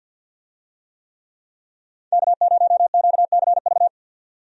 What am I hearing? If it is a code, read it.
Morse code, heard as a keyed tone, about 50 wpm, 700 Hz. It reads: K0QCV
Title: USA Call Signs - 5 Characters - 50wpm